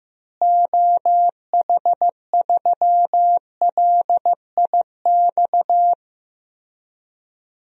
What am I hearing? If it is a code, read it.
Morse code, heard as a keyed tone, about 15 wpm, 700 Hz. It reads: OH3LIX
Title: DXCC-100 Call Signs - 6 Characters - Repeated 2x - 15wpm